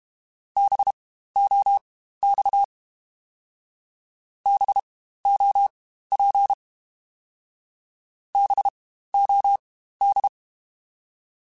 Morse code, heard as a keyed tone, about 32 wpm, spaced as for 15 wpm, 800 Hz. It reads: BOX BOP BOD
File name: Spaced Alphas / Spaced Alphas mp3s / Bs sp @30